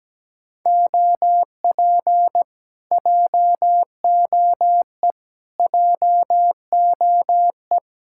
Morse code, heard as a keyed tone, about 17 wpm, 700 Hz. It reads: OP JOE JOE
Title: Speed-Racing Common QSO Phrases - 17wpm